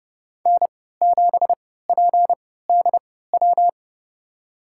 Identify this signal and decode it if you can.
Morse code, heard as a keyed tone, about 30 wpm, spaced as for 17 wpm, 700 Hz. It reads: N7PDW
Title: USA Call Signs - 5 Characters - 3x Character Spacing - 30wpm